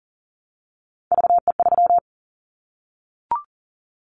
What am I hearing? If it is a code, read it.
Morse code, heard as a keyed tone, about 40 wpm, 700 Hz. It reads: VE3
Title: Callsign Prefixes for Top 25 DXCC - Repeated 3x - 40wpm